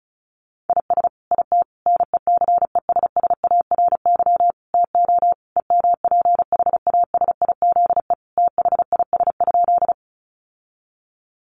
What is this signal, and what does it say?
Morse code, heard as a keyed tone, about 34 wpm, 700 Hz. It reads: IS IT NECESSARY TO EMPHASIZE THIS?
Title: Sentences from Top 1500 Words 35wpm